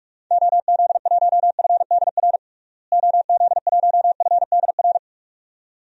Morse code, heard as a keyed tone, about 46 wpm, 700 Hz. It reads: OZ1FDR OZ1FDR
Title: DXCC-100 Call Signs - 6 Characters - Repeated 2x - 45wpm